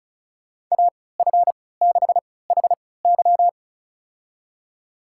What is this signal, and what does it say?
Morse code, heard as a keyed tone, about 34 wpm, spaced as for 20 wpm, 700 Hz. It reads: AF6HY
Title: USA Call Signs - 5 Characters - 3x Character Spacing - 35wpm